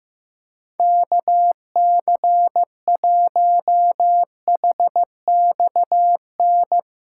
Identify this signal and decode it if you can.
Morse code, heard as a keyed tone, about 15 wpm, 700 Hz. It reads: KC1HXN
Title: USA Callsign DE Callsign - 6 Characters - 15wpm